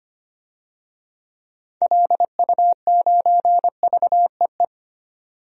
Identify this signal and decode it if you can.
Morse code, heard as a keyed tone, about 25 wpm, 700 Hz. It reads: LU9VEE
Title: DXCC-100 Call Signs - 6 Characters - 25wpm